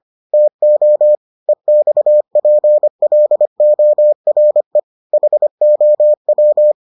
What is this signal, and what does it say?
Morse code, heard as a keyed tone, about 25 wpm, 600 Hz. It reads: TO EXPLORE HOW